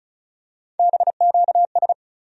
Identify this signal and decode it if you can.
Morse code, heard as a keyed tone, about 36 wpm, 700 Hz. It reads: BQS
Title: USA Call Sign Postfixes - 3 Letters 35wpm